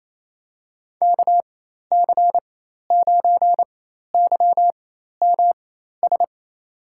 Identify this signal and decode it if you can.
Morse code, heard as a keyed tone, about 28 wpm, spaced as for 13 wpm, 700 Hz. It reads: KC9YMS
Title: USA Call Signs - 6 Characters - 4x Character Spacing - 28wpm